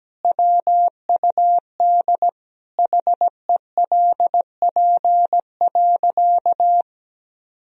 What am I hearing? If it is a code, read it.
Morse code, heard as a keyed tone, about 17 wpm, 700 Hz. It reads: WUD HELP.
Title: Sentences from Top 200 Words and Abbreviations 17wpm